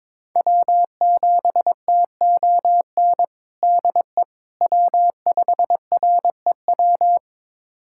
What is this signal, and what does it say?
Morse code, heard as a keyed tone, about 21 wpm, 700 Hz. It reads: W7TON DE W5REW
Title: USA Callsign DE Callsign - 5 Characters - 22wpm